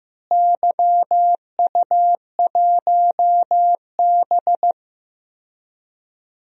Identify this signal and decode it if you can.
Morse code, heard as a keyed tone, about 16 wpm, 700 Hz. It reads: YU1B
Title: DXCC-100 Call Signs - 4 Characters - Repeated 3x - 15wpm